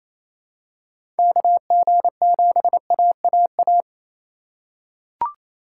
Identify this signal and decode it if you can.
Morse code, heard as a keyed tone, about 28 wpm, 700 Hz. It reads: KG7AAA
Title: USA Call Signs - 2x Character Spacing - 28wpm